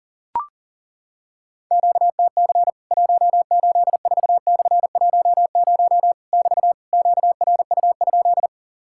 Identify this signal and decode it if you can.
Morse code, heard as a keyed tone, about 37 wpm, 700 Hz. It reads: QTC 184/10 <BT> QRU?
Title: Worked All Europe DX Contest - QTC Exchange - 40wpm